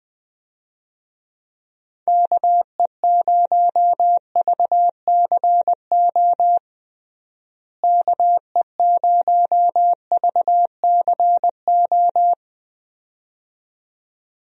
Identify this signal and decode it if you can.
Morse code, heard as a keyed tone, about 21 wpm, 700 Hz. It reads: KE0VCO KE0VCO
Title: USA Call Signs - 6 Characters - Repeated 2x - 20wpm